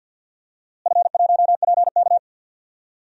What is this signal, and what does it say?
Morse code, heard as a keyed tone, about 46 wpm, 700 Hz. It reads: F1PK
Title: DXCC-100 Call Signs - 4 Characters - Repeated 3x - 50wpm